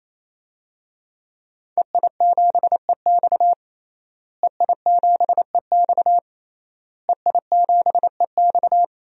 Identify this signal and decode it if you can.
Morse code, heard as a keyed tone, about 28 wpm, 700 Hz. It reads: EI7EX EI7EX EI7EX
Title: DXCC-100 Call Signs - 5 Characters - Repeated 3x - 28wpm